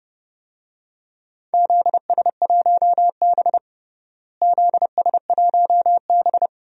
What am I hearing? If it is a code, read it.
Morse code, heard as a keyed tone, about 30 wpm, 700 Hz. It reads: ZS1B ZS1B
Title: DXCC-100 Call Signs - 4 Characters - Repeated 2x - 30wpm